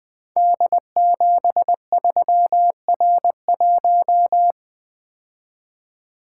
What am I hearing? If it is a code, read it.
Morse code, heard as a keyed tone, about 21 wpm, 700 Hz. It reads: D73R1
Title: Code Groups - 5 Letter:Numbers - 20wpm